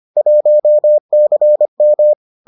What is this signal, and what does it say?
Morse code, heard as a keyed tone, about 25 wpm, 600 Hz. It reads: 1CM